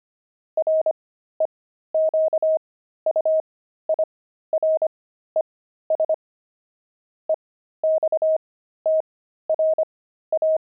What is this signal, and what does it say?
Morse code, heard as a keyed tone, about 25 wpm, spaced as for 13 wpm, 630 Hz. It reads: REQUIRES EXTRA
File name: pangram-fs-13-cs-25-zeal